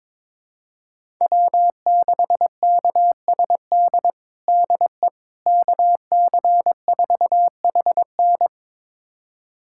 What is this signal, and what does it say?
Morse code, heard as a keyed tone, about 22 wpm, 700 Hz. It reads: W6KSD DE KC4HN
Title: USA Callsign DE Callsign - 5 Characters - 22wpm